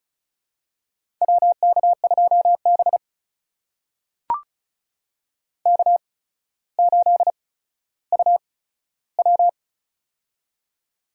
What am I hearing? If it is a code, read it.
Morse code, heard as a keyed tone, about 36 wpm, spaced as for 15 wpm, 700 Hz. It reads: WK2B K8UW
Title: USA Call Signs - 4 Characters - 8x Character Spacing - 35wpm